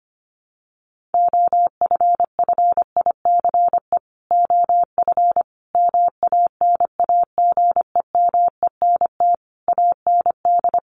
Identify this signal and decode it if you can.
Morse code, heard as a keyed tone, about 25 wpm, 700 Hz. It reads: OFFICE OF MANAGEMENT AND
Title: Sets of 5 Words Encore2 25wpm